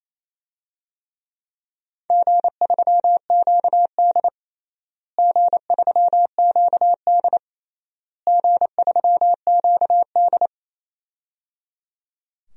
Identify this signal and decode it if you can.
Morse code, heard as a keyed tone, about 28 wpm, 700 Hz. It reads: G3QD G3QD G3QD
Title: DXCC-100 Call Signs - 4 Characters - Repeated 3x - 28wpm